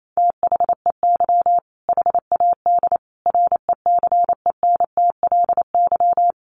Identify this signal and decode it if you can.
Morse code, heard as a keyed tone, about 28 wpm, 700 Hz. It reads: THEY HAD RECENTLY
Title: Sentences from Top 1200 Words 28wpm